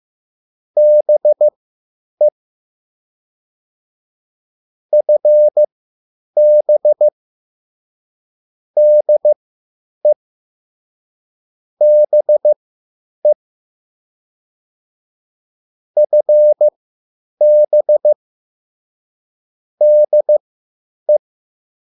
Morse code, heard as a keyed tone, about 15 wpm, spaced as for 9 wpm, 600 Hz. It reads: BE FB DE BE FB DE